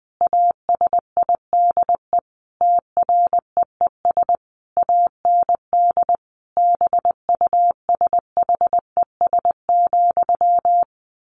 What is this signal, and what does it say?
Morse code, heard as a keyed tone, about 20 wpm, 700 Hz. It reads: ASIDE TREES AND BUSHES,
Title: A Sound Of Thunder 20wpm